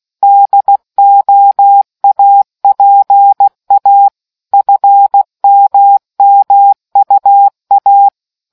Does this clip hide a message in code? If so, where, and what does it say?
Yes: Morse code from 0.2 to 8.1 s, heard as a keyed tone, about 16 wpm, 800 Hz. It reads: DOAPA FMMUA